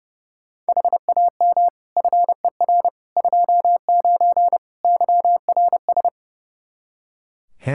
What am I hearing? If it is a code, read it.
Morse code, heard as a keyed tone, about 30 wpm, 700 Hz. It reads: HAM FER 29 YRS
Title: Practice QSO - Exchange 3 - More Info - 30wpm